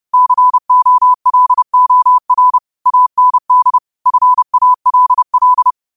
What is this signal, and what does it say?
Morse code, heard as a keyed tone, about 30 wpm, 1000 Hz. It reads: COLOR AND FALL
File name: qotdfiles / 30wpm